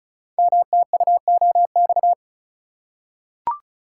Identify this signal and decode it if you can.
Morse code, heard as a keyed tone, about 36 wpm, 700 Hz. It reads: MTUOX
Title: Code Groups - 5 Letters - 35wpm